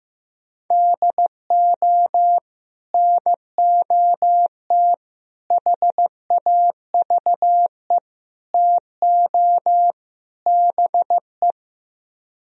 Morse code, heard as a keyed tone, about 15 wpm, 700 Hz. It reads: DO NOT HAVE TO BE
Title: Speed-Racing - Sets of 5 Words 15wpm